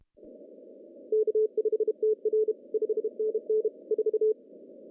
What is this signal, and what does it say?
Morse code, heard as a keyed tone, about 31 wpm, 430 Hz. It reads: K5TR 5NN 4
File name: CW / cqww-cw-k5tr